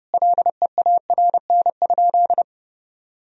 Morse code, heard as a keyed tone, about 32 wpm, 700 Hz. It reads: LEARN?
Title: Sentences from Top 300 Words and Abbreviations 30wpm